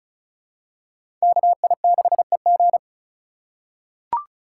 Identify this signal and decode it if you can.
Morse code, heard as a keyed tone, about 34 wpm, 700 Hz. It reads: KI6EG
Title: USA Call Signs - 5 Characters - 2x Character Spacing - 35wpm